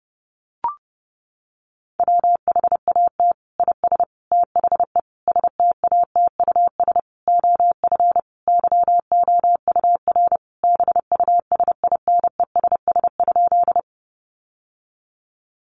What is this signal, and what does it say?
Morse code, heard as a keyed tone, about 30 wpm, 700 Hz. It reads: WHAT IS THE STATUS OF YOUR BUSINESS?
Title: Sentences from Top 1400 Words 30wpm